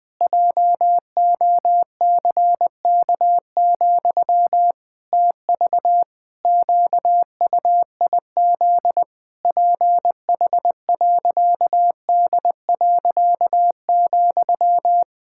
Morse code, heard as a keyed tone, about 20 wpm, 700 Hz. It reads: JOCK, TV QUIZ PH.D.,
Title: Pangram Challenge v2 - 20wpm